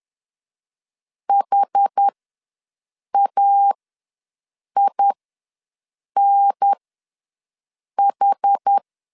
Morse code, heard as a keyed tone, about 10 wpm, spaced as for 6 wpm, 800 Hz. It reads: HAINH
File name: LEZIONI / 5 LBVC / 5 L ws 10wpm 3636